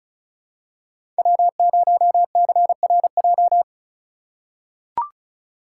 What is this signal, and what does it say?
Morse code, heard as a keyed tone, about 36 wpm, 700 Hz. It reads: W0CRJ
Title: USA Call Signs - 4x Character Spacing - 35wpm